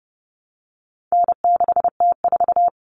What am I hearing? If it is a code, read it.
Morse code, heard as a keyed tone, about 30 wpm, 700 Hz. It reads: N6T4
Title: All Cut Numbers - 4-Digit 30wpm